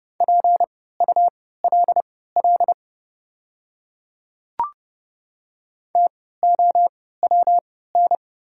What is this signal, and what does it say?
Morse code, heard as a keyed tone, about 30 wpm, spaced as for 17 wpm, 700 Hz. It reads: PULL TOWN
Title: 3x Character Spacing - Top 500 Words - 30wpm